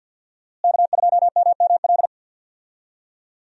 Morse code, heard as a keyed tone, about 50 wpm, 700 Hz. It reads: K2MML
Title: USA Call Signs - 5 Characters - Repeated 3x - 50wpm